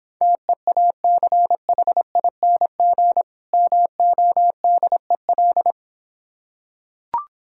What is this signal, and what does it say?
Morse code, heard as a keyed tone, about 26 wpm, 700 Hz. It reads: TEACHING MODEL